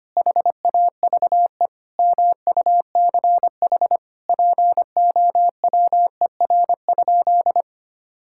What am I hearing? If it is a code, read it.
Morse code, heard as a keyed tone, about 25 wpm, 700 Hz. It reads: HAVE MUCH POWER?
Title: Sentences from Top 1000 Words - Review Part 5 - 25wpm